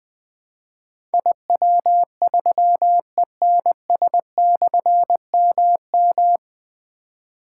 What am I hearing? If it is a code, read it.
Morse code, heard as a keyed tone, about 21 wpm, 700 Hz. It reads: IW3ENS/MM
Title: Difficult Call Signs 20wpm